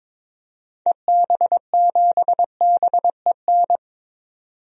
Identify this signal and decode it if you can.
Morse code, heard as a keyed tone, about 22 wpm, 700 Hz. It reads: EB7BEN
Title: DXCC-100 Call Signs - 6 Characters - 22wpm